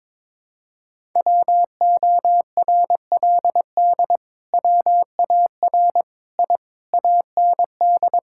Morse code, heard as a keyed tone, about 22 wpm, 700 Hz. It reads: WORLD WAR I AND
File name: practice / 22wpm / Sets of 7 Words - 22wpm